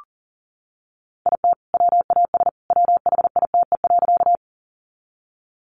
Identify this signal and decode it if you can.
Morse code, heard as a keyed tone, about 40 wpm, 700 Hz. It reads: IT WAS WHITE.
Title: Sentences from Top 300 Words 40wpm